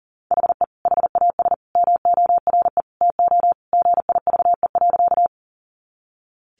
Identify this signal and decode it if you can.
Morse code, heard as a keyed tone, about 40 wpm, 700 Hz. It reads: HE HAS MORE TO GIVE.
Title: Sentences from Top 200 Words 40wpm